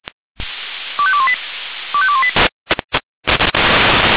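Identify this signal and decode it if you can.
Five-tone selective call (ZVEI), heard as quick runs of five tones, tones of about 70 ms. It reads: 25218 25218
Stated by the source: ZVEI five-tone selective call received by SDR.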